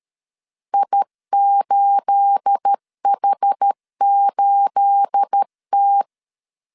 Morse code, heard as a keyed tone, about 13 wpm, 800 Hz. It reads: I8H8T
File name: LEZIONI / 9 28 12wpm 2351